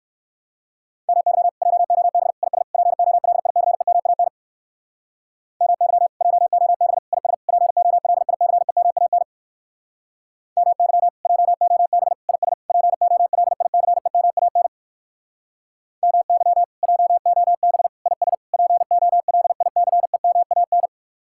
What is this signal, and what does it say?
Morse code, heard as a keyed tone, about 59 wpm, 700 Hz. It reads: MY JOB IS POLICEMAN MY JOB IS POLICEMAN MY JOB IS POLICEMAN MY JOB IS POLICEMAN